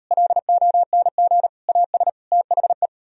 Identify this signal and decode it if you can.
Morse code, heard as a keyed tone, about 38 wpm, 700 Hz. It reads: LONG AS THE